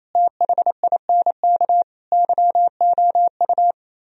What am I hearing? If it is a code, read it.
Morse code, heard as a keyed tone, about 34 wpm, 700 Hz. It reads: THINK YOU
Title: Speed-Racing - Sets of 5 Words 28wpm